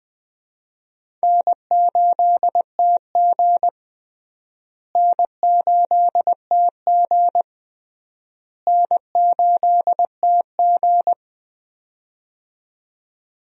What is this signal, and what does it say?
Morse code, heard as a keyed tone, about 21 wpm, 700 Hz. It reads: N8TG N8TG N8TG
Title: USA Call Signs - 4 Characters - Repeated 3x - 20wpm